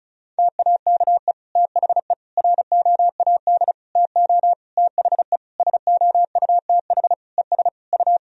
Morse code, heard as a keyed tone, about 36 wpm, 700 Hz. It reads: TAKE THE ROAD TO THE SOUTH ES U
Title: Sentences from Top 400 Words and Abbreviations 35wpm